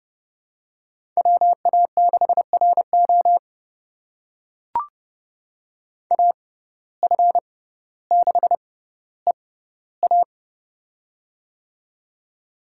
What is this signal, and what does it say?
Morse code, heard as a keyed tone, about 30 wpm, spaced as for 15 wpm, 700 Hz. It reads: WA6RO AF6EA
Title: USA Call Signs - 5 Characters - 6x Character Spacing - 30wpm